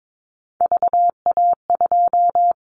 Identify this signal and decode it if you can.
Morse code, heard as a keyed tone, about 22 wpm, 700 Hz. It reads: VA2
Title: Callsign Prefixes for Top 25 DXCC - 22wpm